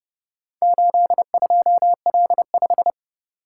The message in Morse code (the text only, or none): 82L5